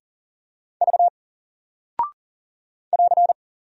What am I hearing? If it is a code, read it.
Morse code, heard as a keyed tone, about 40 wpm, 700 Hz. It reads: V <AR>
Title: Single Character (Letter:Number:Punctuation:Prosign) Alt 40wpm